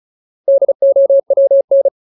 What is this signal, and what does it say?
Morse code, heard as a keyed tone, about 34 wpm, 550 Hz. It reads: DOWN
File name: Ultimate Word List / 35WPM Ultimate